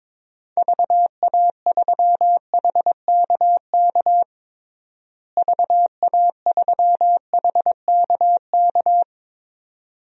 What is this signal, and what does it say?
Morse code, heard as a keyed tone, about 22 wpm, 700 Hz. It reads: VA3HKK VA3HKK
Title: DXCC-100 Call Signs - 6 Characters - Repeated 2x - 22wpm